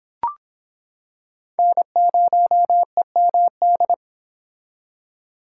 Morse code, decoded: N0EMD

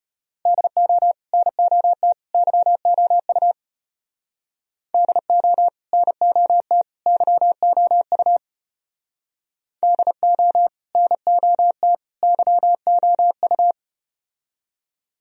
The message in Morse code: DO NOT YOU DO NOT YOU DO NOT YOU

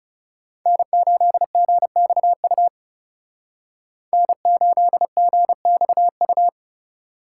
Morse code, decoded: N8GXU N8GXU

